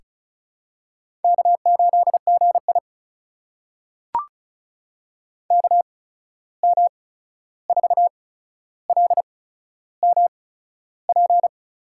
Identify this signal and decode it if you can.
Morse code, heard as a keyed tone, about 36 wpm, spaced as for 13 wpm, 700 Hz. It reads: K8GI KM4LMP
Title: USA Call Signs - 8x Character Spacing - 35wpm